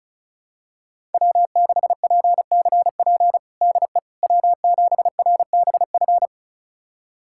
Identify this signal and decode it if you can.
Morse code, heard as a keyed tone, about 36 wpm, 700 Hz. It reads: W6PCP DE W7RBF
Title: USA Callsign DE Callsign - 5 Characters - 35wpm